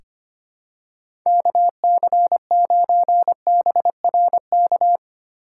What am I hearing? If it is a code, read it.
Morse code, heard as a keyed tone, about 25 wpm, 700 Hz. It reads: KC9BRK
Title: USA Call Signs - 6 Characters - 4x Character Spacing - 25wpm